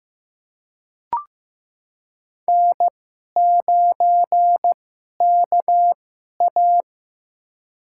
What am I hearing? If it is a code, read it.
Morse code, heard as a keyed tone, about 15 wpm, spaced as for 11 wpm, 700 Hz. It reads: N9KA
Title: USA Call Signs - 2x Character Spacing - 15wpm